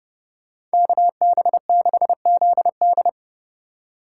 Morse code, heard as a keyed tone, about 29 wpm, 700 Hz. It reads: KB6ZD
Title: USA Call Signs - 5 Characters - 4x Character Spacing - 30wpm